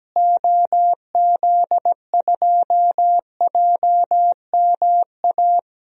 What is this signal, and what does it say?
Morse code, heard as a keyed tone, about 17 wpm, 700 Hz. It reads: OZ2JMA
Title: DXCC-100 Call Signs - 6 Characters - Repeated 3x - 17wpm